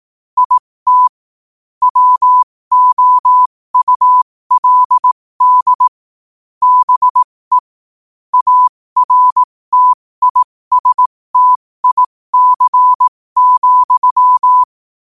Morse code, heard as a keyed tone, about 18 wpm, spaced as for 15 wpm, 1000 Hz. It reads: IT WOULD BE ARTISTIC,
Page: New quotes every day in morse code at 15 Words per minute.